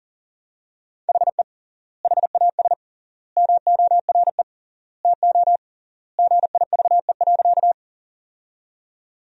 Morse code, heard as a keyed tone, about 40 wpm, 700 Hz. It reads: HE HAS MORE TO GIVE.
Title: 3x Word-Spacing - Sentences from Top 200 Words - 40wpm